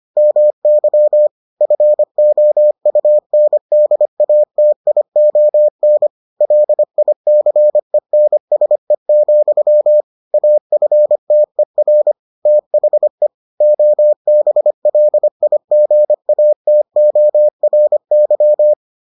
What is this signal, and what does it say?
Morse code, heard as a keyed tone, about 25 wpm, 600 Hz. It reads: MY FOUNDATION LICENSE, AFTER THE OBLIGATORY